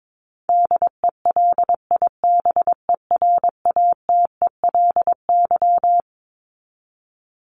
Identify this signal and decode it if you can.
Morse code, heard as a keyed tone, about 22 wpm, 700 Hz. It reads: DELIBERATELY
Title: Top 1751-2000 Words 22wpm